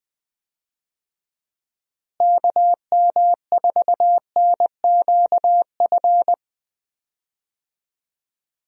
Morse code, heard as a keyed tone, about 21 wpm, 700 Hz. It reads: KM4NQF
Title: USA Call Signs - 6 Characters - Repeated 3x - 20wpm